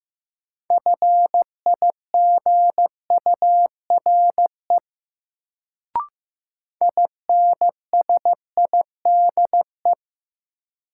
Morse code, heard as a keyed tone, about 15 wpm, 700 Hz. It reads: FIGURE INSIDE
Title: Sentences from Top 1000 Words - Review Part 2 - New Word Drill - 15wpm